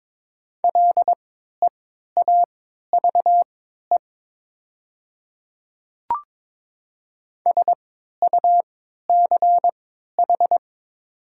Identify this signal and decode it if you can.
Morse code, heard as a keyed tone, about 22 wpm, spaced as for 12 wpm, 700 Hz. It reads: LEAVE SUCH
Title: 3x Character Spacing - Top 500 Words - 22wpm